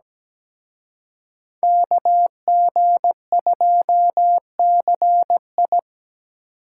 Morse code, heard as a keyed tone, about 20 wpm, 700 Hz. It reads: KG2CI
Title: Speed Racing - USA Call Signs 17wpm